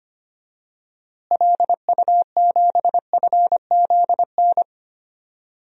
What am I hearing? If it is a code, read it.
Morse code, heard as a keyed tone, about 25 wpm, 700 Hz. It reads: LU7FZN
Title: DXCC-100 Call Signs - 6 Characters - Repeated 2x - 25wpm